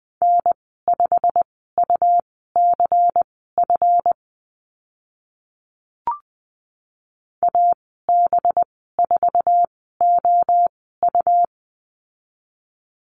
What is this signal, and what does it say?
Morse code, heard as a keyed tone, about 20 wpm, spaced as for 14 wpm, 700 Hz. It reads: N5UCF AB4OU
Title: 186 Lesson 30g - Letter b - Call Signs 20wpm